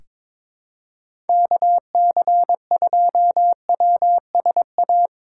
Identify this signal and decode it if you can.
Morse code, heard as a keyed tone, about 22 wpm, 700 Hz. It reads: KC2WSA
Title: USA Call Signs - 6x Character Spacing - 22wpm